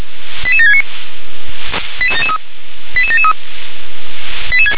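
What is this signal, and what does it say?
Five-tone selective call (ZVEI), heard as quick runs of five tones, tones of about 70 ms. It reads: 88769 88773
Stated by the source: ZVEI five-tone selective call 88713, received by SDR.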